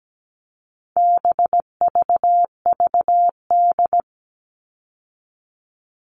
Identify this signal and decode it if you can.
Morse code, heard as a keyed tone, about 17 wpm, 700 Hz. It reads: BVVD
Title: All Cut Numbers - 4-Digit 17wpm